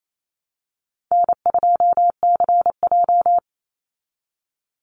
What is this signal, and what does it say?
Morse code, heard as a keyed tone, about 28 wpm, 700 Hz. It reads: N2CJ
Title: Call Signs Repeated 3x - 28wpm